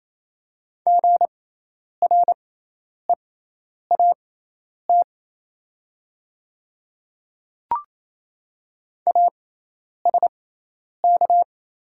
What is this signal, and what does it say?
Morse code, heard as a keyed tone, about 28 wpm, spaced as for 10 wpm, 690 Hz. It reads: GREAT ASK
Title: 6x Character Spacing - Top 200 Words - 28wpm